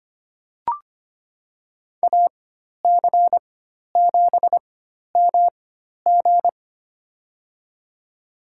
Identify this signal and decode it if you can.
Morse code, heard as a keyed tone, about 25 wpm, spaced as for 12 wpm, 700 Hz. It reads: AC7MG
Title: USA Call Signs - 5 Characters - 4x Character Spacing - 25wpm